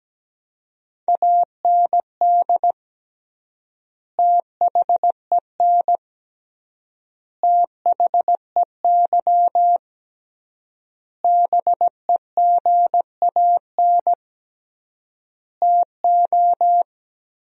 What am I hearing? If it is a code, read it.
Morse code, heard as a keyed tone, about 17 wpm, 700 Hz. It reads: AND THEN THEY BEGAN TO
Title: 3x Word-Spacing - Sentences from Top 300 Words - 17wpm